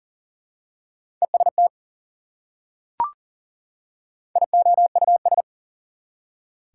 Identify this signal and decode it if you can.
Morse code, heard as a keyed tone, about 40 wpm, 700 Hz. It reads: EST IOUS
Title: Prefixes and Suffixes 40wpm